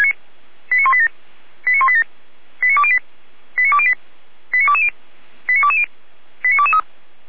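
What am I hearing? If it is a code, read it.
Five-tone selective call (ZVEI), heard as quick runs of five tones, tones of about 70 ms. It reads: 89187 89187 89208 89208 89220 89220 89223
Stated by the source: ZVEI five-tone selective call for address 89240, received by SDR at feuerwehr-orlamuende (Orlamünde fire brigade).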